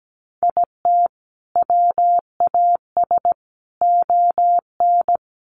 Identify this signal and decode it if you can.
Morse code, heard as a keyed tone, about 17 wpm, 700 Hz. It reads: IT WAS ON
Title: Sentences from Top 800 Words 17wpm